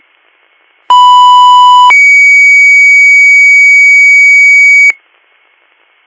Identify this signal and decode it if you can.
Two-tone page, heard as a one-second tone, then a three-second tone, about 990 then 2250 Hz. It is heard from 0.9 to 4.9 s.